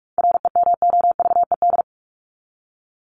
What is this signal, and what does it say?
Morse code, heard as a keyed tone, about 44 wpm, 700 Hz. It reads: REMOVED
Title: ED Suffix - Top 100 Words with Base Words 45wpm